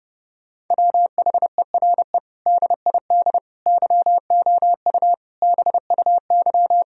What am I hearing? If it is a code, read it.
Morse code, heard as a keyed tone, about 32 wpm, 700 Hz. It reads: WHERE DID YOU BUY
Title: Sentences from Top 1000 Words - Review Part 4 - 30wpm